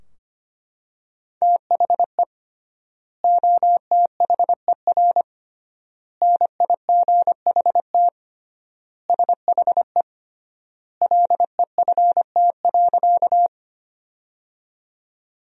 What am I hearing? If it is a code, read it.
Morse code, heard as a keyed tone, about 25 wpm, 700 Hz. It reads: THE OTHER NIGHT SHE LEFT.
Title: 3x Word-Spacing - Sentences from Top 300 Words - 25wpm